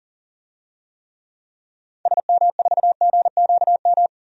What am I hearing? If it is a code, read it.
Morse code, heard as a keyed tone, about 40 wpm, 700 Hz. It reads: SM4GQM
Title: DXCC-100 Call Signs - 6 Characters - 40wpm